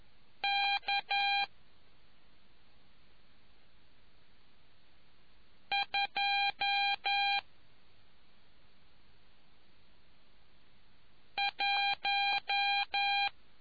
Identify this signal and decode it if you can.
Morse code, heard as a keyed tone, about 11 wpm, spaced as for 2 wpm, 800 Hz. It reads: K 21